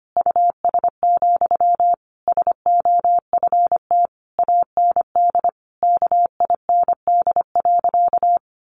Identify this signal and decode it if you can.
Morse code, heard as a keyed tone, about 25 wpm, 700 Hz. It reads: US, SOFT AND KIND.
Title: Sentences from Top 1500 Words 25wpm